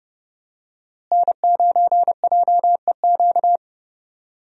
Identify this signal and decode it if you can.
Morse code, heard as a keyed tone, about 30 wpm, 700 Hz. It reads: N9JEQ
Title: USA Call Signs - 5 Characters - Repeated 3x - 30wpm